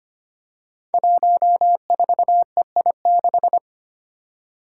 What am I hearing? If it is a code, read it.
Morse code, heard as a keyed tone, about 25 wpm, 700 Hz. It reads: 14EI6
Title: Five Challenging Letter-Number Combinations 25wpm